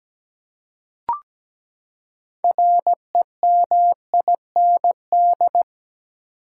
Morse code, heard as a keyed tone, about 17 wpm, 700 Hz. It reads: REMIND
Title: Six Letter Words - 17wpm